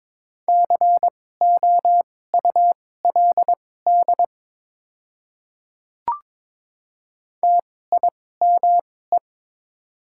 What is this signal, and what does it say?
Morse code, heard as a keyed tone, about 22 wpm, spaced as for 16 wpm, 700 Hz. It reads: COULD TIME
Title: 2x Character Spacing - Top 100 Words - 22wpm